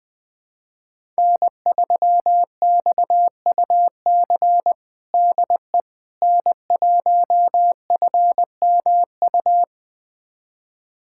Morse code, heard as a keyed tone, about 20 wpm, 700 Hz. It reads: N3XUC DE N1FMU
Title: USA Callsign DE Callsign - 5 Characters - 20wpm